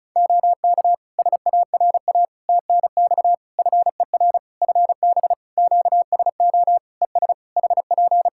Morse code, heard as a keyed tone, about 34 wpm, 700 Hz. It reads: OK SARA TNX FER FB QSO ES HP
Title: Practice QSO - Exchange 4 - Ending - 35wpm